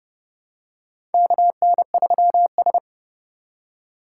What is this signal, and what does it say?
Morse code, heard as a keyed tone, about 30 wpm, 700 Hz. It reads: KN3S
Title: USA Call Signs - 4 Characters - 3x Character Spacing - 30wpm